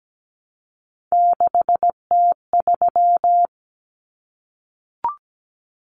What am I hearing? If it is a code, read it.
Morse code, heard as a keyed tone, about 17 wpm, 700 Hz. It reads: 6T3
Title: Common Cut Numbers - 3-Digit 17wpm